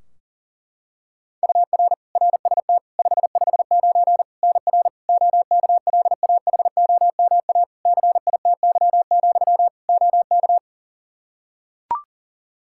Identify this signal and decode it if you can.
Morse code, heard as a keyed tone, about 40 wpm, 700 Hz. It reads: UR RST 559 NR OKLAHOMA CITY, OK
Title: Practice QSO - Exchange 1 - Beginning - 40wpm